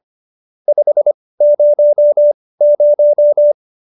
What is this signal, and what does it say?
Morse code, heard as a keyed tone, about 25 wpm, spaced as for 18 wpm, 590 Hz. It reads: 500